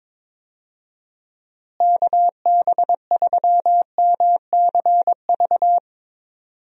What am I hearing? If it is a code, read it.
Morse code, heard as a keyed tone, about 22 wpm, 700 Hz. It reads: KB3MCV
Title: USA Call Signs - 6 Characters - Repeated 2x - 22wpm